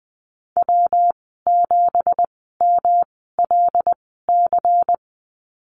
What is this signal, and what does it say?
Morse code, heard as a keyed tone, about 20 wpm, spaced as for 15 wpm, 700 Hz. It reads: W7MLC
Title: 207 Lesson 33g - Letter j - Call Signs 20wpm